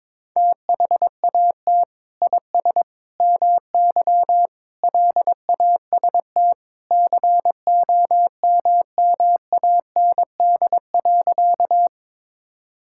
Morse code, heard as a keyed tone, about 22 wpm, 700 Hz. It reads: THAT IS MY LAST COMMAND.
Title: Sentences from Top 1600 Words 22wpm